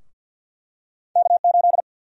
Morse code, heard as a keyed tone, about 52 wpm, 700 Hz. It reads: K8
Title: USA Call Sign Prefixes - Repeated 2x - 50wpm